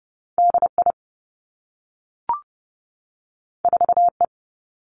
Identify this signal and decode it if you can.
Morse code, heard as a keyed tone, about 30 wpm, 700 Hz. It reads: D I 4 E
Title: Two Challenging Letter-Number Combinations 30wpm